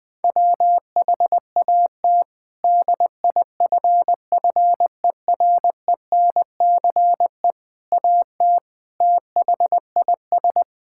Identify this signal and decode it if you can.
Morse code, heard as a keyed tone, about 20 wpm, 700 Hz. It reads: WHAT DIFFERENCE AT THIS
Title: Sets of 8 Words - 20wpm